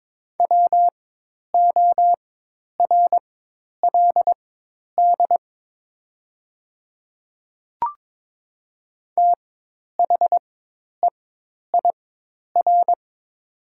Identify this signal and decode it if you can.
Morse code, heard as a keyed tone, about 22 wpm, spaced as for 10 wpm, 700 Hz. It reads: WORLD THEIR
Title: 4x Character Spacing - Top 200 Words - 22wpm